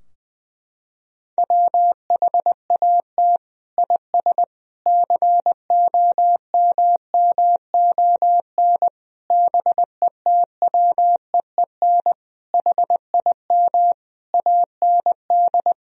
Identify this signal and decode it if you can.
Morse code, heard as a keyed tone, about 21 wpm, 700 Hz. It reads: WHAT IS COMMON BETWEEN HIM AND
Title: Sentences from Top 1000 Words - Review Part 3 - 20wpm